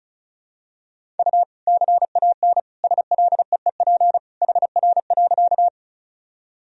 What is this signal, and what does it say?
Morse code, heard as a keyed tone, about 36 wpm, 700 Hz. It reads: U CAN SLEEP HR.
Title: Sentences from Top 500 Words and Abbreviations 35wpm